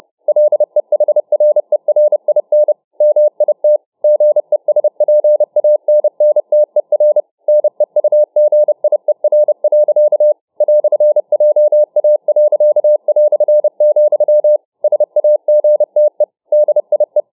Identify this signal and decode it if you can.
Morse code, heard as a keyed tone, about 29 wpm, 600 Hz. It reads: LEHRERIN MIT GESPANNTER NEUGIER. "JA.", SAGTE DIE